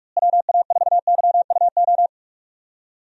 Morse code, heard as a keyed tone, about 46 wpm, 700 Hz. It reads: WA4YUO
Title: USA Call Signs - 6 Characters - Repeated 2x - 45wpm